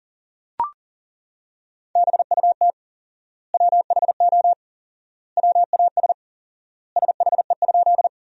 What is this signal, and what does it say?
Morse code, heard as a keyed tone, about 37 wpm, 700 Hz. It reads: BUT WHO WAS SHE?
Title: 4x Word-Spacing - Sentences from Top 100 Words - 40wpm